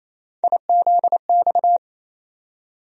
Morse code, heard as a keyed tone, about 28 wpm, 700 Hz. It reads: IZX
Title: USA Call Sign Postfixes - 3 Letters 28wpm